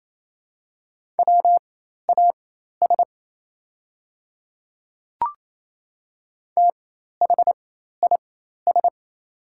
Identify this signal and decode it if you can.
Morse code, heard as a keyed tone, about 28 wpm, spaced as for 13 wpm, 700 Hz. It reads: WAS THIS
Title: 4x Character Spacing - Top 100 Words - 28wpm